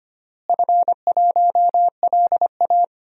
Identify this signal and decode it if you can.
Morse code, heard as a keyed tone, about 25 wpm, 700 Hz. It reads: F1LA
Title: DXCC-100 Call Signs - 4 Characters - Repeated 3x - 25wpm